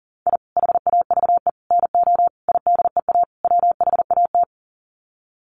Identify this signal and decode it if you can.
Morse code, heard as a keyed tone, about 40 wpm, 700 Hz. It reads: I HAVE NO IDEA WHAT
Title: Sets of 5 Words Encore 40wpm